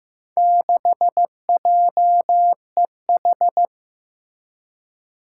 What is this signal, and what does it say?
Morse code, heard as a keyed tone, about 15 wpm, 700 Hz. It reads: 6JEH
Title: Four Challenging Letter-Number Combinations 15wpm